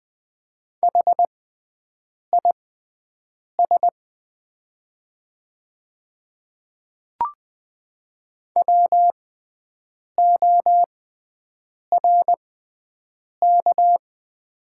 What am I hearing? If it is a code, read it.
Morse code, heard as a keyed tone, about 20 wpm, spaced as for 7 wpm, 700 Hz. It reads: HIS WORK
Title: 6x Character Spacing - Top 200 Words - 20wpm